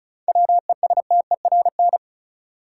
Morse code, heard as a keyed tone, about 36 wpm, 700 Hz. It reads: WESTERN
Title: Sentences from Top 1000 Words - Review Part 7 - New Word Drill - 35wpm